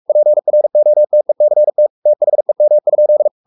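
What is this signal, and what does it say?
Morse code, heard as a keyed tone, about 44 wpm, 600 Hz. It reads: PROTECT THEM?
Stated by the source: Morse code at 43 wpm sending Ken Follett's novel Never in English, from CW med Gnister.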